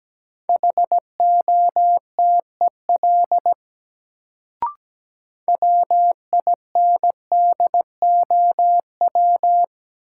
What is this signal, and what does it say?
Morse code, heard as a keyed tone, about 17 wpm, 700 Hz. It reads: HOTEL WINDOW
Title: Sentences from Top 1600 Words - New Word Drill - 17wpm